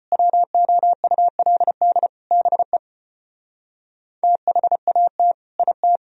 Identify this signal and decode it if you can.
Morse code, heard as a keyed tone, about 33 wpm, 690 Hz. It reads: WOULD BE THAT IT